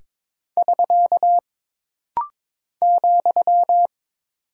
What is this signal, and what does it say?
Morse code, heard as a keyed tone, about 22 wpm, 700 Hz. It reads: <SK> ,
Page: This rapid-fire timing is geared to push your ICR (Instant Character Recognition) skills.